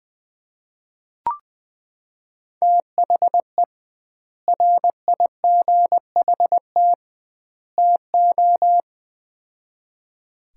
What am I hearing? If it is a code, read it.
Morse code, heard as a keyed tone, about 20 wpm, 700 Hz. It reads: THE RIGHT TO
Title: 2x Word-Spacing - Sets of 3 Words - 20wpm